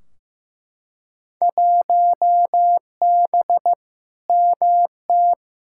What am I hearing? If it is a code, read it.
Morse code, heard as a keyed tone, about 15 wpm, 700 Hz. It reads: 1B MT
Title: Field Day Exchanges - 15wpm